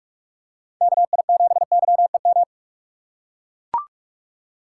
Morse code, heard as a keyed tone, about 41 wpm, 700 Hz. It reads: KI7YEM
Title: USA Call Signs - 6 Characters - 2x Character Spacing - 45wpm